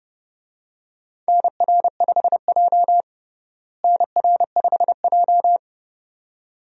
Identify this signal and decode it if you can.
Morse code, heard as a keyed tone, about 32 wpm, 700 Hz. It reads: NR5J NR5J
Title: USA Call Signs - 4 Characters - Repeated 2x - 30wpm